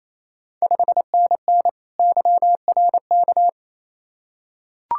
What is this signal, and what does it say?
Morse code, heard as a keyed tone, about 28 wpm, 700 Hz. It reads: 5NN YRK
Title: Ontario QSO Party Exchange 28wpm